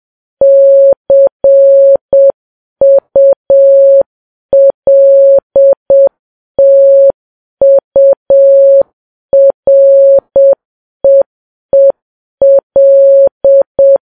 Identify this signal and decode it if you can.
Morse code, heard as a keyed tone, about 7 wpm, 550 Hz. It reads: CULTUREEL
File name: Files Web tls / Morse on9fv / morse7kps